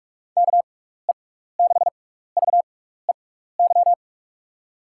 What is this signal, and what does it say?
Morse code, heard as a keyed tone, about 41 wpm, spaced as for 15 wpm, 700 Hz. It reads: KE6VEY
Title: USA Call Signs - 6 Characters - 6x Character Spacing - 45wpm